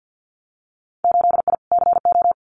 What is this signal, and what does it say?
Morse code, heard as a keyed tone, about 50 wpm, 700 Hz. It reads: 8I CO
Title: Winter Field Day Exchanges Encore 50wpm